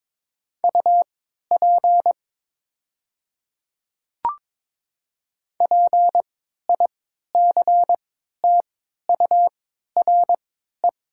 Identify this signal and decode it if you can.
Morse code, heard as a keyed tone, about 22 wpm, spaced as for 12 wpm, 700 Hz. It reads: UP PICTURE
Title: 3x Character Spacing - Top 300 Words - 22wpm